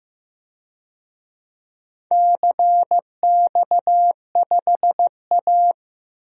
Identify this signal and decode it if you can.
Morse code, heard as a keyed tone, about 15 wpm, 700 Hz. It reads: CX5A
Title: DXCC-100 Call Signs - 4 Characters - Repeated 3x - 15wpm